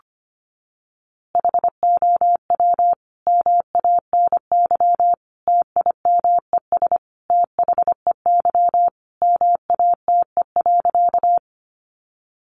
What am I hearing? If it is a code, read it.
Morse code, heard as a keyed tone, about 25 wpm, 700 Hz. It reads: HOW MANY TIMES THEY MATE.
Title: A Sound Of Thunder 25wpm